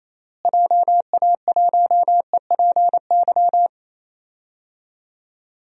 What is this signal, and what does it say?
Morse code, heard as a keyed tone, about 28 wpm, 700 Hz. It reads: JA1EPY
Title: DXCC-100 Call Signs - 6 Characters - Repeated 2x - 28wpm